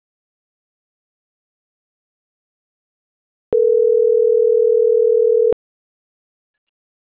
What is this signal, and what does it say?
Ringback tone (North America), 440 + 480 Hz.